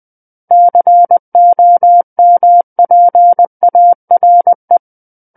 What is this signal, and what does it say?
Morse code, heard as a keyed tone, about 20 wpm, 700 Hz. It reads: COMPARE
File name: 20 WPM